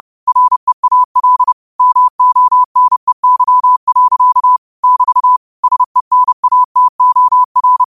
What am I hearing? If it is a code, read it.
Morse code, heard as a keyed tone, about 30 wpm, 1000 Hz. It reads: REAL MONEY. <BT> SENATOR
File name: qotdfiles / 30wpm